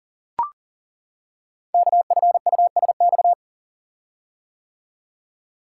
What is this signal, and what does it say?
Morse code, heard as a keyed tone, about 38 wpm, 690 Hz. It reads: KFUSX
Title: Code Groups - 5 Letters - 40wpm